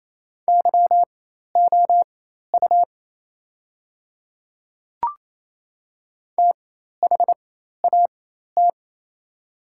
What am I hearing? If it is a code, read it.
Morse code, heard as a keyed tone, about 28 wpm, spaced as for 13 wpm, 690 Hz. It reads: YOU THAT
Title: 4x Character Spacing - Top 300 Words - 28wpm